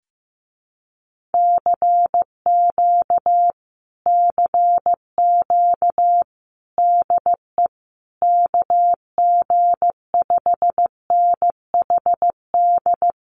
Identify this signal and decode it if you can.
Morse code, heard as a keyed tone, about 15 wpm, 700 Hz. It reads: CQ CQ DE KG5NHD
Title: Common QSO Phrases Encore 15wpm